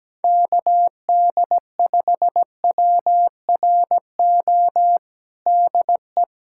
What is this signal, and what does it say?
Morse code, heard as a keyed tone, about 17 wpm, 700 Hz. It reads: KD5WRO DE
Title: USA Callsign DE Callsign - 6 Characters - 17wpm